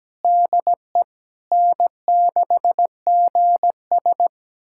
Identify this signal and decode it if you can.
Morse code, heard as a keyed tone, about 17 wpm, 700 Hz. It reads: DE N6GS
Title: USA Callsign DE Callsign - 4 Characters - 17wpm